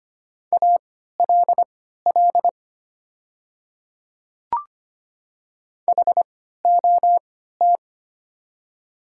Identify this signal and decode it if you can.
Morse code, heard as a keyed tone, about 25 wpm, spaced as for 14 wpm, 700 Hz. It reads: ALL HOT
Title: 3x Character Spacing - Top 200 Words - 25wpm